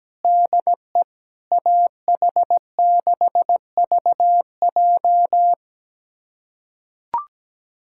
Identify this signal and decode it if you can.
Morse code, heard as a keyed tone, about 17 wpm, 700 Hz. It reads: DE AH6VJ
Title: USA Callsign DE Callsign - 5 Characters - 17wpm